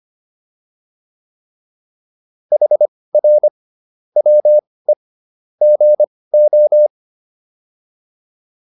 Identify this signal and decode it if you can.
Morse code, heard as a keyed tone, about 25 wpm, spaced as for 18 wpm, 600 Hz. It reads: HR WE GO
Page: intermediate-practice-files